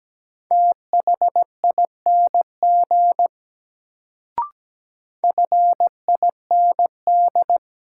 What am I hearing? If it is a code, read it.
Morse code, heard as a keyed tone, about 17 wpm, 700 Hz. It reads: THING FIND
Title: Sentences from Top 100 Words - New Word Drill - 17wpm